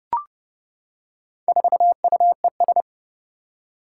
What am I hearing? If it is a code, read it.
Morse code, heard as a keyed tone, about 30 wpm, 690 Hz. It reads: 4UES
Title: Four Challenging Letter-Number Combinations 30wpm